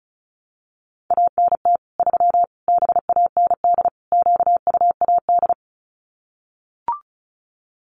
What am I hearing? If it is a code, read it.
Morse code, heard as a keyed tone, about 34 wpm, 700 Hz. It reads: ANT 3 BAND QUAD
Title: Common QSO Phrases 35wpm